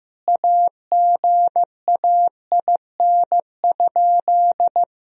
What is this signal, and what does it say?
Morse code, heard as a keyed tone, about 15 wpm, 700 Hz. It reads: AGAIN?
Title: Sentences from Top 1000 Words - Review Part 1 - 15wpm